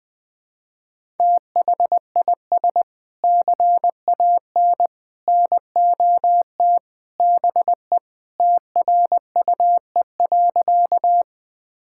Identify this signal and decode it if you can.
Morse code, heard as a keyed tone, about 21 wpm, 700 Hz. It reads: THIS CAN NOT BE TRUE.
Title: Sentences with 3-Word Phrases - 20wpm